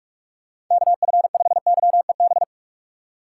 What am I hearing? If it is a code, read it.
Morse code, heard as a keyed tone, about 46 wpm, 700 Hz. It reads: KF5YEB
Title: USA Call Signs - 6x Character Spacing - 45wpm